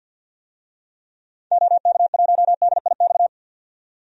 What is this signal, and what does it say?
Morse code, heard as a keyed tone, about 46 wpm, 700 Hz. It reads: OK1DIX
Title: DXCC-100 Call Signs - 6 Characters - 50wpm